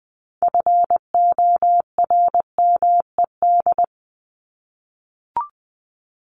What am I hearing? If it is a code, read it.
Morse code, heard as a keyed tone, about 20 wpm, 700 Hz. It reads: FORMED
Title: ED Suffix - Top 100 Words with Base Words 20wpm